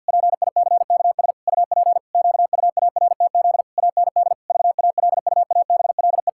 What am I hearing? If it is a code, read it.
Morse code, heard as a keyed tone, about 50 wpm, 700 Hz. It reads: PICKS UP QUARTZ AND VALUABLE
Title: Pangram Challenge v2 - 50wpm